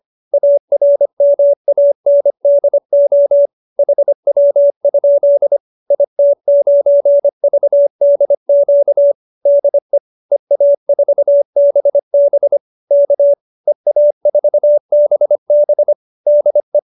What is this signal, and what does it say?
Morse code, heard as a keyed tone, about 25 wpm, 590 Hz. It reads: ARMANDO HW? IT9VDQ DE EA4BB K EA4BB DE